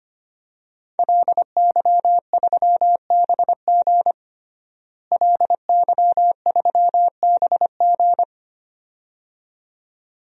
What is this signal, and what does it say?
Morse code, heard as a keyed tone, about 25 wpm, 700 Hz. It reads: LY3BG LY3BG
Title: DXCC-100 Call Signs - 5 Characters - Repeated 2x - 25wpm